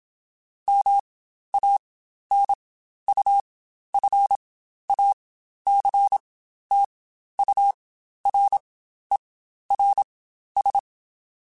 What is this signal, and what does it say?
Morse code, heard as a keyed tone, about 27 wpm, spaced as for 12 wpm, 800 Hz. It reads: MANUFACTURERS
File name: Quotes Spaced / Quotes Spaced-mp3s / Airtight sp @25